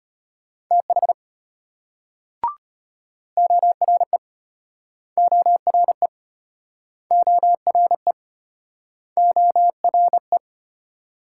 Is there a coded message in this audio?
Yes: Morse code from 0.7 to 10.4 s, heard as a keyed tone, about 32 wpm, 700 Hz. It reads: TH ORE ORE ORE ORE